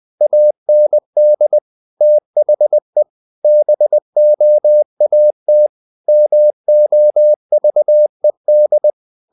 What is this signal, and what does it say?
Morse code, heard as a keyed tone, about 20 wpm, 600 Hz. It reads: AND THE BOAT MOVED
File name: the old man and the sea part 2 20wpm